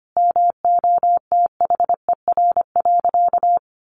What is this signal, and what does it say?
Morse code, heard as a keyed tone, about 25 wpm, 700 Hz. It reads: MOTHER.
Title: Sentences from Top 1400 Words 25wpm